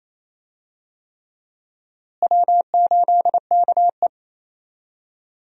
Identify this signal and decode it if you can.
Morse code, heard as a keyed tone, about 28 wpm, 700 Hz. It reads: W8KE
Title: USA Call Signs - 4 Characters - Repeated 3x - 28wpm